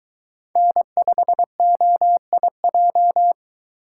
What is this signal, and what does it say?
Morse code, heard as a keyed tone, about 20 wpm, 700 Hz. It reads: N5OIJ